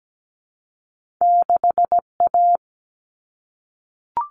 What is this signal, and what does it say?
Morse code, heard as a keyed tone, about 17 wpm, 700 Hz. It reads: 6A
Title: Common Cut Numbers - 2-Digit 17wpm